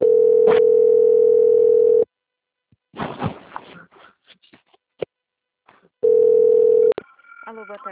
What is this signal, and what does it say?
Ringback tone (North America), 440 + 480 Hz.